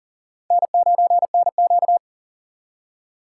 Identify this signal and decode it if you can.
Morse code, heard as a keyed tone, about 54 wpm, spaced as for 47 wpm, 700 Hz. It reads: N9NQ
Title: Speed Racing - USA 1x2 Call Signs 40wpm